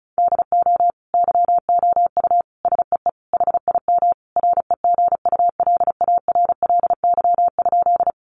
Morse code, heard as a keyed tone, about 34 wpm, 700 Hz. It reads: DO YOU SEE HIM REGULARLY?
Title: Sentences from Top 1400 Words 35wpm